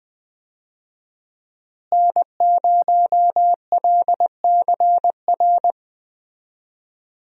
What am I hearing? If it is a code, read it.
Morse code, heard as a keyed tone, about 21 wpm, 700 Hz. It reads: N0LCR
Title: USA Call Signs - 5 Characters - Repeated 2x - 20wpm